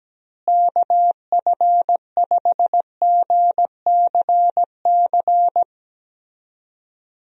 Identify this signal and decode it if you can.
Morse code, heard as a keyed tone, about 17 wpm, 700 Hz. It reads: KF5GCC
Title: USA Call Signs - 6 Characters - Repeated 2x - 17wpm